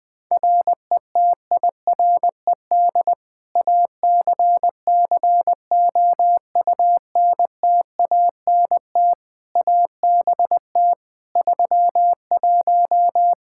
Morse code, heard as a keyed tone, about 21 wpm, 700 Hz. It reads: RETIRED ACCOUNTANT ABT 31
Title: K1USN SST QSO Practice 20wpm